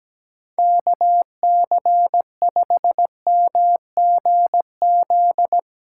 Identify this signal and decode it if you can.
Morse code, heard as a keyed tone, about 17 wpm, 700 Hz. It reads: KC5MGZ
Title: Speed Racing - USA 2x3 Call Signs 17wpm